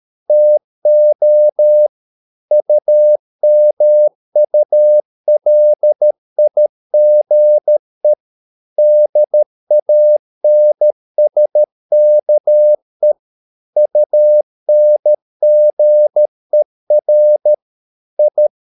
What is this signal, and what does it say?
Morse code, heard as a keyed tone, about 13 wpm, 600 Hz. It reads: TO UMULIGE DANSKE UNGER I
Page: Ferie 13 wpm